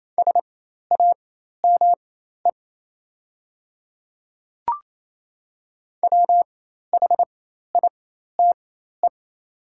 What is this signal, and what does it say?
Morse code, heard as a keyed tone, about 28 wpm, spaced as for 13 wpm, 700 Hz. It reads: SAME WHITE
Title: 4x Character Spacing - Top 300 Words - 28wpm